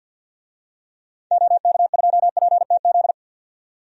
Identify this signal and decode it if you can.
Morse code, heard as a keyed tone, about 51 wpm, 700 Hz. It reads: OK2PTZ
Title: DXCC-100 Call Signs - 6 Characters - 50wpm